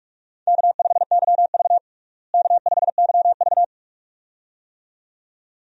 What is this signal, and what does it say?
Morse code, heard as a keyed tone, about 46 wpm, 700 Hz. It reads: K5YV K5YV
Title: USA Call Signs - 4 Characters - Repeated 2x - 45wpm